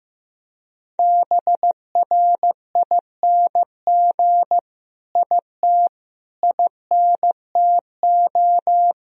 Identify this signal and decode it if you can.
Morse code, heard as a keyed tone, about 15 wpm, 700 Hz. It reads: BRING IT INTO
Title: Sentences from Top 1000 Words - Review Part 4 - 15wpm